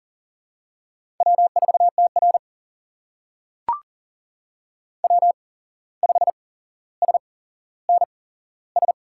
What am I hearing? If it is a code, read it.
Morse code, heard as a keyed tone, about 40 wpm, spaced as for 16 wpm, 700 Hz. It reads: W4TR W5SNS
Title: USA Call Signs - 8x Character Spacing - 40wpm